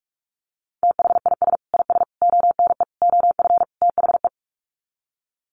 Morse code, heard as a keyed tone, about 44 wpm, 700 Hz. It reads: THIS IS ONE OF THE
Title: Sets of 5 Words 45wpm